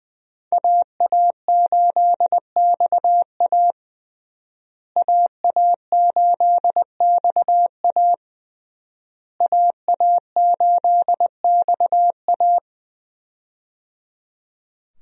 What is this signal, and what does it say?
Morse code, heard as a keyed tone, about 20 wpm, 700 Hz. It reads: AA8XA AA8XA AA8XA
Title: USA Call Signs - 5 Characters - Repeated 3x - 20wpm